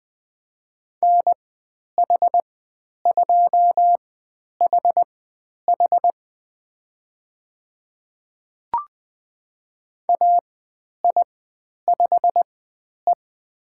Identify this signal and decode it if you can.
Morse code, heard as a keyed tone, about 20 wpm, spaced as for 10 wpm, 700 Hz. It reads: NH2HH AI5E
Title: 095 Lesson 16g - Letter c - Call Signs 20wpm